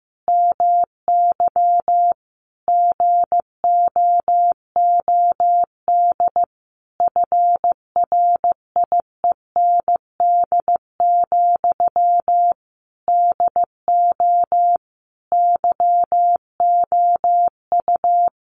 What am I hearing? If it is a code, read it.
Morse code, heard as a keyed tone, about 15 wpm, 700 Hz. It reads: MY GOOD FRIEND, DO YOU
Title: Sentences from Top 1100 Words 15wpm